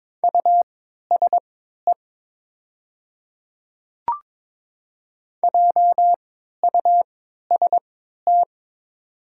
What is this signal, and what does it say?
Morse code, heard as a keyed tone, about 22 wpm, spaced as for 12 wpm, 700 Hz. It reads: USE JUST
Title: 3x Character Spacing - Top 200 Words - 22wpm